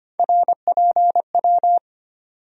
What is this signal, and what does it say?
Morse code, heard as a keyed tone, about 25 wpm, 700 Hz. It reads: RPW
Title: USA Call Sign Postfixes - 3 Letters 25wpm